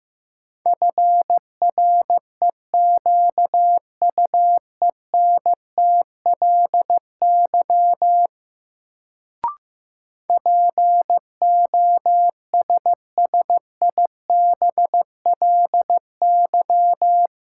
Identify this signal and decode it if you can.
Morse code, heard as a keyed tone, about 15 wpm, 700 Hz. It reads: FREQUENTLY POSSIBLY
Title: Sentences from Top 1300 Words - New Word Drill Alt - 15wpm